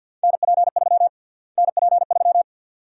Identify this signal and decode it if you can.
Morse code, heard as a keyed tone, about 51 wpm, 700 Hz. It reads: NP3 NP3
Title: USA Call Sign Prefixes - Repeated 2x - 50wpm